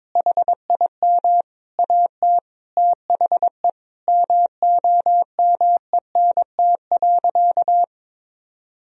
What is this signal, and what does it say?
Morse code, heard as a keyed tone, about 22 wpm, 700 Hz. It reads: HIM AT THE MOMENT.
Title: Sentences from Top 1000 Words - Review Part 3 - 22wpm